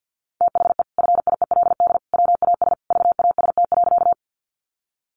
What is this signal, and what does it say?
Morse code, heard as a keyed tone, about 50 wpm, 700 Hz. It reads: THE FIELD WAS VAST.
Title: Sentences from Top 1300 Words 50wpm